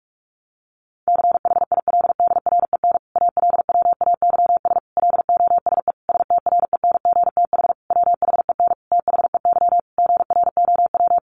Morse code, heard as a keyed tone, about 46 wpm, 700 Hz. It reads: CHILDREN ALWAYS LOSE STRENGTH WHEN THEY GROW
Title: Sentences from Top 1300 Words 45wpm